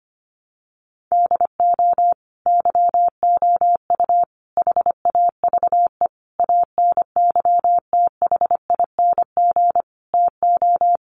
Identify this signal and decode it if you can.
Morse code, heard as a keyed tone, about 25 wpm, 700 Hz. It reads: DO YOU HAVE ANYTHING TO
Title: Sentences from Top 1000 Words 25wpm